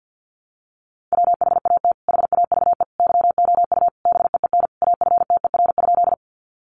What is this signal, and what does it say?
Morse code, heard as a keyed tone, about 50 wpm, 700 Hz. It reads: WHAT HAVE YOU BEEN AFTER?
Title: Sentences from Top 200 Words 50wpm